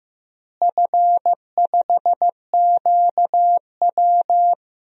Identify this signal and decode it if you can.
Morse code, heard as a keyed tone, about 15 wpm, 700 Hz. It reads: F5QW
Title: DXCC-100 Call Signs - 4 Characters - Repeated 3x - 15wpm